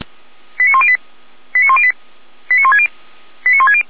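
Five-tone selective call (ZVEI), heard as quick runs of five tones, tones of about 70 ms. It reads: 89108 89108 89160 89160
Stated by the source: ZVEI five-tone call 89223 received by SDR, from feuerwehr-orlamuende.org (fire brigade Orlamünde).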